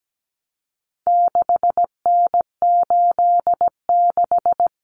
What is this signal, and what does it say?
Morse code, heard as a keyed tone, about 17 wpm, 700 Hz. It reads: 6N86
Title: Common Cut Numbers - 4-Digit 17wpm